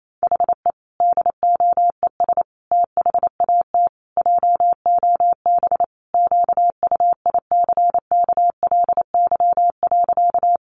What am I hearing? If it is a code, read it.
Morse code, heard as a keyed tone, about 28 wpm, 700 Hz. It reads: HE DOES THAT JOB QUICKLY.
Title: Sentences from Top 700 Words 28wpm